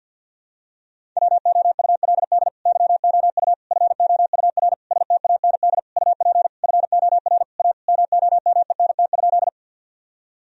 Morse code, heard as a keyed tone, about 52 wpm, 700 Hz. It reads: WOULD YOU FOUR STAND UP FOR A MOMENT?
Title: Sentences from Top 1000 Words - Review Part 1 - 50wpm